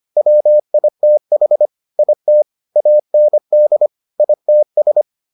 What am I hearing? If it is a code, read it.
Morse code, heard as a keyed tone, about 25 wpm, 600 Hz. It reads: WITH IT AND ITS